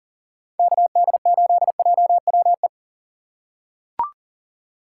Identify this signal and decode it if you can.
Morse code, heard as a keyed tone, about 39 wpm, 700 Hz. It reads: KD8JWE
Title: USA Call Signs - 3x Character Spacing - 40wpm